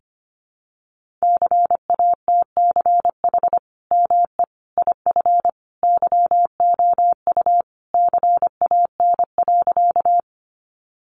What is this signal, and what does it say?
Morse code, heard as a keyed tone, about 25 wpm, 700 Hz. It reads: CATCH ME IF YOU CAN.
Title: Sentences from Top 1500 Words 25wpm